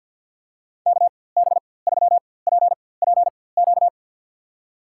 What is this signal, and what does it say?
Morse code, heard as a keyed tone, about 52 wpm, spaced as for 24 wpm, 700 Hz. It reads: KB3PPQ
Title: USA Call Signs - 6 Characters - 4x Character Spacing - 50wpm